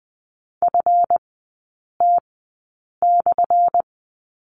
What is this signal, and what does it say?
Morse code, heard as a keyed tone, about 20 wpm, spaced as for 8 wpm, 700 Hz. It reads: FT/
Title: 178 Lesson 29e - Slash - Three Characters 20wpm